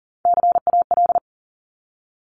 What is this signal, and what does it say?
Morse code, heard as a keyed tone, about 40 wpm, 700 Hz. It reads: CAL
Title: Top Three Letter Combinations in Words 40wpm